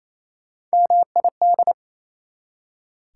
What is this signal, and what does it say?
Morse code, heard as a keyed tone, about 38 wpm, spaced as for 34 wpm, 690 Hz. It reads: MID
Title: Speed-Racing Prefixes and Suffixes 28wpm